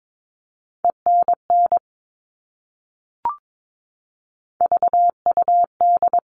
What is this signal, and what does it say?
Morse code, heard as a keyed tone, about 22 wpm, 700 Hz. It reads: ENN VUD
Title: All Cut Numbers - 3-Digit 22wpm